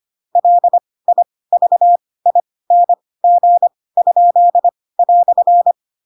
Morse code, heard as a keyed tone, about 25 wpm, spaced as for 18 wpm, 700 Hz. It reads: LIVING?"
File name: Pinocchio / Pinocchio ch4 12 25 ik1hge